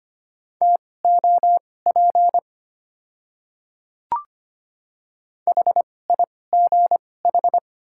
Morse code, heard as a keyed tone, about 25 wpm, spaced as for 18 wpm, 700 Hz. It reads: TOP HIGH
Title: 2x Character Spacing - Top 500 Words - 25wpm